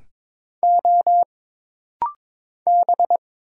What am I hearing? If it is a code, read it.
Morse code, heard as a keyed tone, about 22 wpm, 700 Hz. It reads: O B
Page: This rapid-fire timing is geared to push your ICR (Instant Character Recognition) skills.